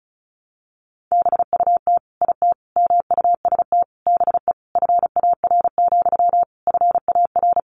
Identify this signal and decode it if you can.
Morse code, heard as a keyed tone, about 34 wpm, 700 Hz. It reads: BUT IT MUST BE FAR, FAR
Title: Sentences from Top 600 Words 35wpm